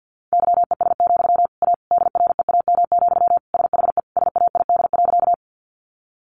Morse code, heard as a keyed tone, about 50 wpm, 700 Hz. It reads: YES, A DREAM, SHE SAID.
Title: Sentences from Top 1500 Words 50wpm